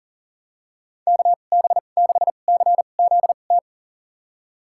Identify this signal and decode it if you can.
Morse code, heard as a keyed tone, about 37 wpm, spaced as for 28 wpm, 700 Hz. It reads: KB6CZT
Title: USA Call Signs - 6 Characters - 2x Character Spacing - 40wpm